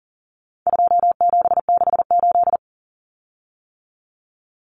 Morse code, heard as a keyed tone, about 40 wpm, 700 Hz. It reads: 2768
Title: Common Cut Numbers - 4-Digit 40wpm